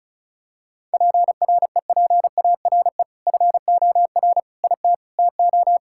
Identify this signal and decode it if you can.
Morse code, heard as a keyed tone, about 34 wpm, 700 Hz. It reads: PREPARE FOR IT TO
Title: Sentences from Top 1000 Words - Review Part 6 - 35wpm